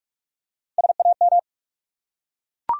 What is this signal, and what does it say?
Morse code, heard as a keyed tone, about 42 wpm, 700 Hz. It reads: SAM
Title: USA Common Names - Max 3 Letters - 45wpm